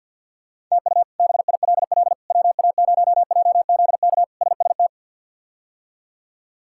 Morse code, heard as a keyed tone, about 52 wpm, 700 Hz. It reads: TU BILL WA0JZK SST
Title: K1USN SST QSO Practice - 50wpm